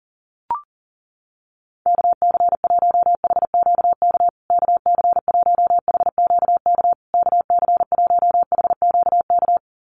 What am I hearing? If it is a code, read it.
Morse code, heard as a keyed tone, about 40 wpm, 700 Hz. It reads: KC1HQK KC1HQK KC1HQK
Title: USA Call Signs Repeated 3x 40wpm